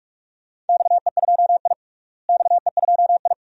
Morse code, heard as a keyed tone, about 46 wpm, 700 Hz. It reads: XE2I XE2I
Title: DXCC-100 Call Signs - 4 Characters - Repeated 2x - 45wpm